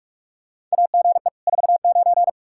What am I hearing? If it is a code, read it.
Morse code, heard as a keyed tone, about 41 wpm, 700 Hz. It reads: AGE 49
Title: Practice QSO - Exchange 3 - More Info - 45wpm